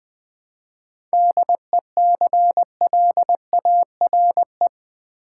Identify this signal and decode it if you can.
Morse code, heard as a keyed tone, about 20 wpm, 690 Hz. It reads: DECLARE
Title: 7 Letter Words - First 3 Letters Given - 20wpm